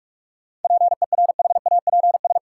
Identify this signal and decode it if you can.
Morse code, heard as a keyed tone, about 46 wpm, 700 Hz. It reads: PERHAPS
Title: Sentences from Top 500 Words - New Word Drill - 45wpm